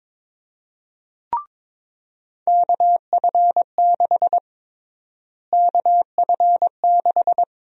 Morse code, heard as a keyed tone, about 22 wpm, 700 Hz. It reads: KF6 KF6
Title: USA Call Sign Prefixes - Repeated 2x - 22wpm